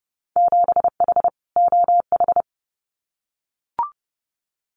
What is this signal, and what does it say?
Morse code, heard as a keyed tone, about 30 wpm, 700 Hz. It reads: 7H OH
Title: Winter Field Day Exchanges Encore 30wpm